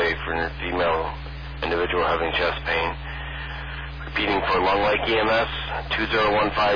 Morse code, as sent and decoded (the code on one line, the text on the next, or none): none